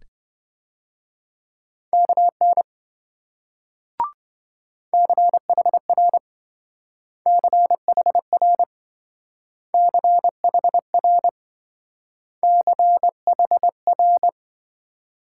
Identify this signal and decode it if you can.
Morse code, heard as a keyed tone, about 27 wpm, 700 Hz. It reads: KN CHR CHR CHR CHR